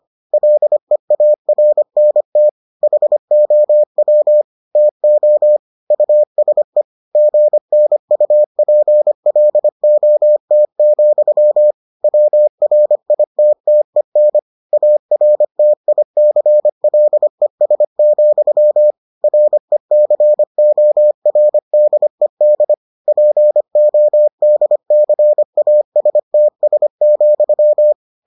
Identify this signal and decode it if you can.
Morse code, heard as a keyed tone, about 25 wpm, 600 Hz. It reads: LEARNT HOW TO USE GNUPLOT, WRITTEN ARTICLES, RECORDED PODCASTS,